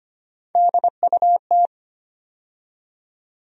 Morse code, heard as a keyed tone, about 25 wpm, 700 Hz. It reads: DUT
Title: USA Call Sign Postfixes - 3 Letters 25wpm